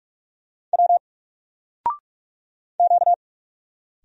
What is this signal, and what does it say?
Morse code, heard as a keyed tone, about 42 wpm, 700 Hz. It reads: W Q
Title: Single Letter-Number 45wpm